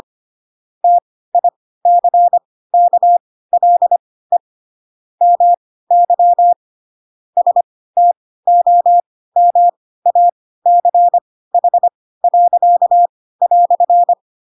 Morse code, heard as a keyed tone, about 25 wpm, spaced as for 16 wpm, 700 Hz. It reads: TICKLE MY STOMACH."
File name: Pinocchio / Pinocchio ch1 10 25 ik1hge